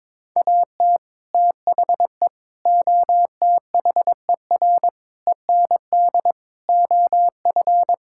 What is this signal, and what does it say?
Morse code, heard as a keyed tone, about 21 wpm, 700 Hz. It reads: AT THE OTHER END OF
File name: practice / 22wpm / Build A Phrase - 5-Word Phrases - 22wpm